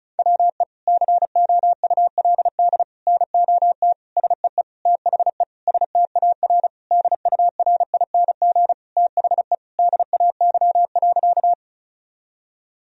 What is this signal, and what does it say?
Morse code, heard as a keyed tone, about 36 wpm, 700 Hz. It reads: WE COULD NOT SEE THE STAR DURING THE DAY.
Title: Sentences from Top 1000 Words - Review Part 2 - 35wpm